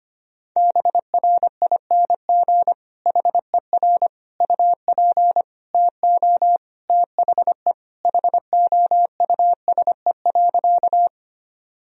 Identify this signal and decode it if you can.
Morse code, heard as a keyed tone, about 25 wpm, 700 Hz. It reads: BRING HER UP TO THE HOUSE.
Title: Sentences from Top 1000 Words - Review Part 3 - 25wpm